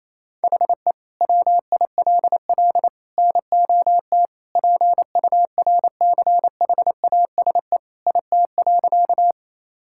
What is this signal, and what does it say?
Morse code, heard as a keyed tone, about 28 wpm, 700 Hz. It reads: HE WILL NOT PURCHASE IT.
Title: Sentences from Top 1000 Words - Review Part 5 - 28wpm